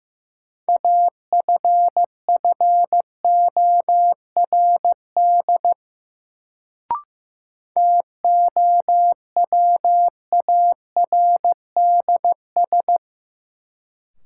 Morse code, heard as a keyed tone, about 15 wpm, 700 Hz. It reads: AFFORD TOWARDS
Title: Sentences from Top 1100 Words - New Word Drill Alt - 15wpm